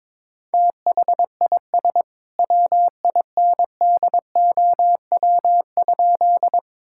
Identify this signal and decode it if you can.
Morse code, heard as a keyed tone, about 22 wpm, 700 Hz. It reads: THIS WINDOW?
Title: Sentences from Top 1600 Words 22wpm